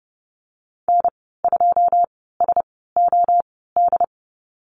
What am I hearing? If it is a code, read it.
Morse code, heard as a keyed tone, about 30 wpm, spaced as for 17 wpm, 700 Hz. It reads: N2SOD
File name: Lesson 14g - Number 2 - Call Signs 30wpm